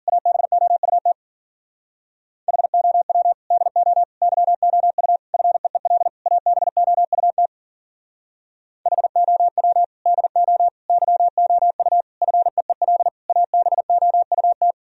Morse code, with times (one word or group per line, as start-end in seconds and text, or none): 0.1-1.1 ABOUT
2.5-3.3 HOW
3.5-4.0 DO
4.2-5.2 YOU
5.3-6.1 FEEL
6.3-7.5 ABOUT
8.9-9.8 HOW
10.1-10.7 DO
10.9-12.0 YOU
12.2-13.1 FEEL
13.3-14.7 ABOUT